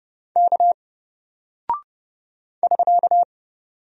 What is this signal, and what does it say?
Morse code, heard as a keyed tone, about 30 wpm, 700 Hz. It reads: K <SK>
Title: Single Character (Letter:Number:Punctuation:Prosign) Alt 30wpm